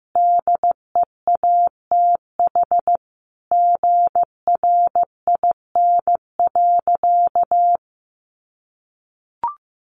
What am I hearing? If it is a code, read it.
Morse code, heard as a keyed tone, about 15 wpm, 700 Hz. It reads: DEATH GRIN.
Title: A Sound Of Thunder 15wpm